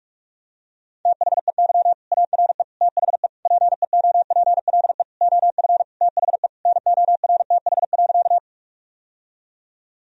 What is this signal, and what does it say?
Morse code, heard as a keyed tone, about 46 wpm, 700 Hz. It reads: THEY ARE THE PEOPLE OF THE NORTH.
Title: Sentences from Top 1000 Words - Review Part 1 - 45wpm